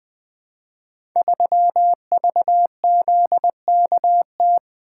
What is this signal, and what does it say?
Morse code, heard as a keyed tone, about 20 wpm, 700 Hz. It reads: 3VZKT
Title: Code Groups - 5 Letter:Numbers - 20wpm